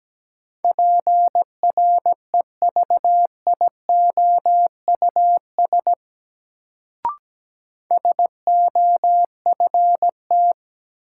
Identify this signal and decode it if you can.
Morse code, heard as a keyed tone, about 17 wpm, 700 Hz. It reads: PREVIOUS SOFT
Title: Sentences from Top 1500 Words - New Word Drill Alt - 17wpm